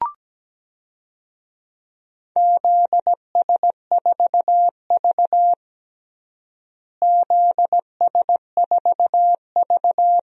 Morse code, heard as a keyed tone, about 17 wpm, 700 Hz. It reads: ZS4V ZS4V
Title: DXCC-100 Call Signs - 4 Characters - Repeated 3x - 17wpm